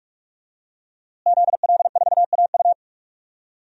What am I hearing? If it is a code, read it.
Morse code, heard as a keyed tone, about 46 wpm, 700 Hz. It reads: ZL4AU
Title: DXCC-100 Call Signs - 5 Characters - Repeated 2x - 45wpm